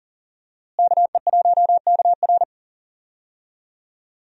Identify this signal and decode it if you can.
Morse code, heard as a keyed tone, about 40 wpm, 700 Hz. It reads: KE1KR
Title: USA Call Signs - 5 Characters - Repeated 2x - 40wpm